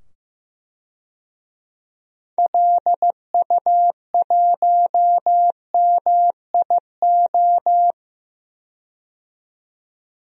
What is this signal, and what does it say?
Morse code, heard as a keyed tone, about 15 wpm, 700 Hz. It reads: LU1MIO
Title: DXCC-100 Call Signs - 6 Characters - 15wpm